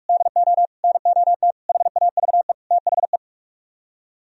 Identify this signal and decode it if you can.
Morse code, heard as a keyed tone, about 44 wpm, 700 Hz. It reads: DO NOT HAVE THE